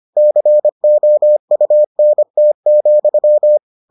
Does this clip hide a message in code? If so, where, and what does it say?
Yes: Morse code from 0.2 to 3.6 s, heard as a keyed tone, about 25 wpm, 600 Hz. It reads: COUNT,